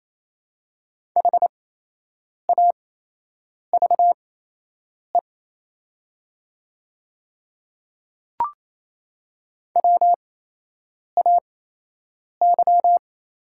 Morse code, heard as a keyed tone, about 28 wpm, spaced as for 8 wpm, 690 Hz. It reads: HAVE WAY
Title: 8x Character Spacing - Top 100 Words - 28wpm